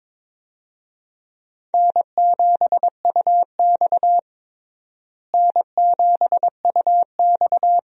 Morse code, heard as a keyed tone, about 23 wpm, 700 Hz. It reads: N7UX N7UX
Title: USA Call Signs - 4 Characters - Repeated 2x - 22wpm